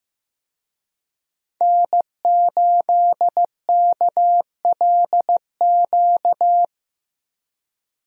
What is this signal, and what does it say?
Morse code, heard as a keyed tone, about 15 wpm, 700 Hz. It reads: N8KLQ
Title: USA Call Signs - 5 Characters - Repeated 3x - 15wpm